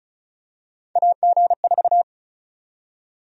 Morse code, heard as a keyed tone, about 36 wpm, 700 Hz. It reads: AG4
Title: USA Call Sign Prefixes - 35wpm